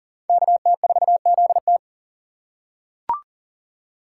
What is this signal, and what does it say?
Morse code, heard as a keyed tone, about 40 wpm, 700 Hz. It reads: KT4ZT
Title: USA Call Signs - 5 Characters - 2x Character Spacing - 40wpm